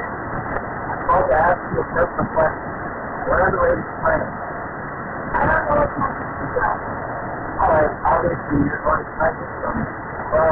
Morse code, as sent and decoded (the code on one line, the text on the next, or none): none